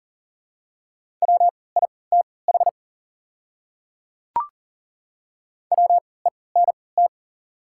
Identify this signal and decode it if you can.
Morse code, heard as a keyed tone, about 40 wpm, spaced as for 23 wpm, 700 Hz. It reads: WITH WENT
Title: 3x Character Spacing - Top 200 Words - 40wpm